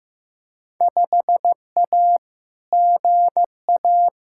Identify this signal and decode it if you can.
Morse code, heard as a keyed tone, about 15 wpm, 700 Hz. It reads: 5A GA
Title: Field Day Exchanges Encore - 15wpm